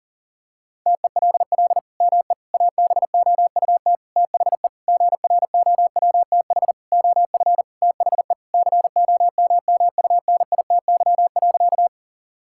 Morse code, heard as a keyed tone, about 40 wpm, 700 Hz. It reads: TELL ME ABOUT THE GROWTH OF THE COMMUNITY.
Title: Sentences from Top 1000 Words - Review Part 5 - 40wpm